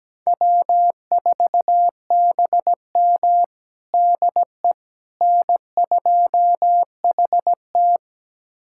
Morse code, heard as a keyed tone, about 17 wpm, 700 Hz. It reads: W4BM DE N2HT
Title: USA Callsign DE Callsign - 4 Characters - 17wpm